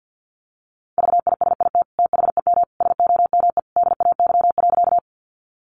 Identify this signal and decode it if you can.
Morse code, heard as a keyed tone, about 50 wpm, 700 Hz. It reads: VISIT THEM SOME DAY.
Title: Sentences from Top 900 Words 50wpm